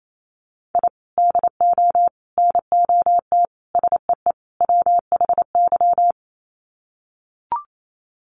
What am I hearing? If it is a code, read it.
Morse code, heard as a keyed tone, about 28 wpm, 690 Hz. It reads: I DO NOT SEE WHY
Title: Sets of 5 Words Encore 28wpm